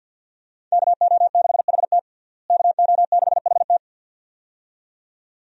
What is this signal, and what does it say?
Morse code, heard as a keyed tone, about 52 wpm, 700 Hz. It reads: KO6HT KO6HT
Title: USA Call Signs - 5 Characters - Repeated 2x - 50wpm